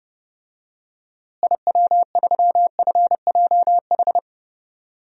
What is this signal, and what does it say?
Morse code, heard as a keyed tone, about 32 wpm, 700 Hz. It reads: IW3FJH
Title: DXCC-100 Call Signs - 6 Characters - Repeated 3x - 30wpm